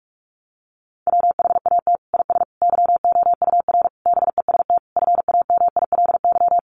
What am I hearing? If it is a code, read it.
Morse code, heard as a keyed tone, about 44 wpm, 700 Hz. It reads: WHAT IS YOUR BEST FAMILY
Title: Sentences from Top 1200 Words 45wpm